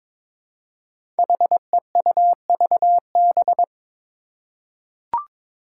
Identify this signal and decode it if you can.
Morse code, heard as a keyed tone, about 22 wpm, 700 Hz. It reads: HEUVB
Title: Five Challenging Letter-Number Combinations 22wpm